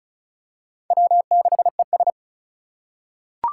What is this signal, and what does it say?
Morse code, heard as a keyed tone, about 36 wpm, 700 Hz. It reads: W6ES
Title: Four Challenging Letter-Number Combinations 35wpm